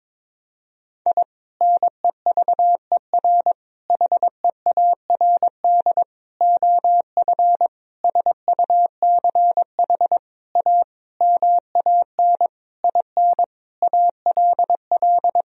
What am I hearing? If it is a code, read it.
Morse code, heard as a keyed tone, about 22 wpm, 700 Hz. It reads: I NEVER HEARD OF SUCH A MAN IN ALL
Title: Sentences from Top 1000 Words - Review Part 3 - 22wpm